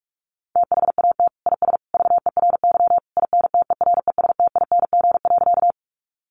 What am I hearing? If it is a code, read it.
Morse code, heard as a keyed tone, about 44 wpm, 700 Hz. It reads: THAT IS VERY INTERESTING.
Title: Sentences from Top 1000 Words 45wpm